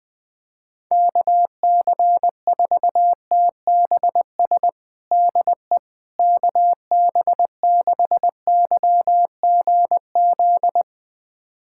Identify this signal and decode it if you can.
Morse code, heard as a keyed tone, about 20 wpm, 700 Hz. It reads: KC4TBS DE KB6YGZ
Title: USA Callsign DE Callsign - 6 Characters - 20wpm